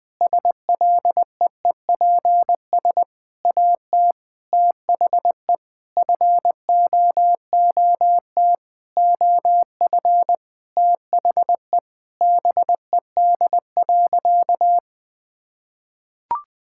Morse code, heard as a keyed tone, about 21 wpm, 700 Hz. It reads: SLEEPS AT THE FOOT OF THE BED.
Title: Sentences with 5-Word Phrases - 20wpm